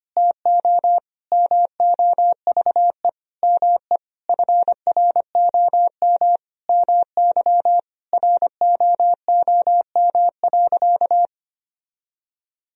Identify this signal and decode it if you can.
Morse code, heard as a keyed tone, about 25 wpm, 700 Hz. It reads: TO MOVE ME FROM MY ROOM.
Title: Sentences from Top 1000 Words - Review Part 1 - 25wpm